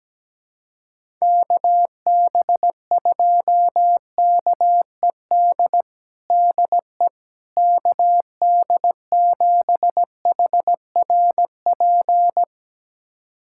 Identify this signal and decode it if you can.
Morse code, heard as a keyed tone, about 17 wpm, 700 Hz. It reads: KB2KED DE KD7HRP
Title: USA Callsign DE Callsign 17wpm